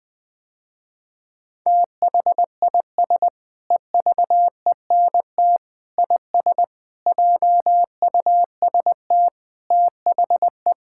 Morse code, heard as a keyed tone, about 21 wpm, 700 Hz. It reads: THIS EVENT IS JUST THE
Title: Sentences with 8-Word Phrases - Spoken Beginning - 20wpm